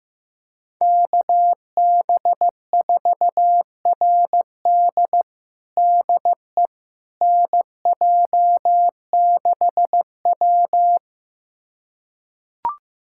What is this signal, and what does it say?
Morse code, heard as a keyed tone, about 15 wpm, 700 Hz. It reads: KB4RD DE NJ6W
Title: USA Callsign DE Callsign 15wpm